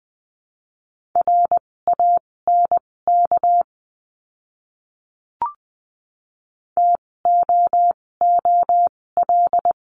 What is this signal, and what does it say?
Morse code, heard as a keyed tone, about 20 wpm, spaced as for 16 wpm, 700 Hz. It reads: RANK TOOL
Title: 256 Lesson 42a - Words 12-20wpm